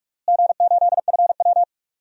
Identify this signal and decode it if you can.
Morse code, heard as a keyed tone, about 46 wpm, 700 Hz. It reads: G8FW
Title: DXCC-100 Call Signs - 4 Characters - Repeated 3x - 45wpm